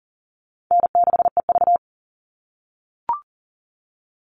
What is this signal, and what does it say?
Morse code, heard as a keyed tone, about 40 wpm, 700 Hz. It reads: N6EV
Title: Easy Call Signs - 40wpm